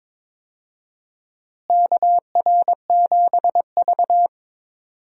Morse code, heard as a keyed tone, about 22 wpm, 700 Hz. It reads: KR7V
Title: USA Call Signs - 4 Characters - Repeated 2x - 22wpm